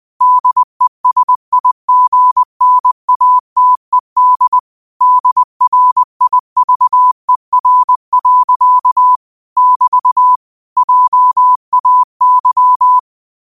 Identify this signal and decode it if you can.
Morse code, heard as a keyed tone, about 20 wpm, 1000 Hz. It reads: DESIGNATED DRIVER. <BT> JAY
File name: qotdfiles / 20wpm